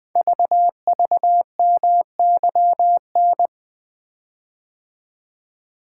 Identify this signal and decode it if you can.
Morse code, heard as a keyed tone, about 20 wpm, 700 Hz. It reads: VVMYN
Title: Code Groups - 5 Letters - 20wpm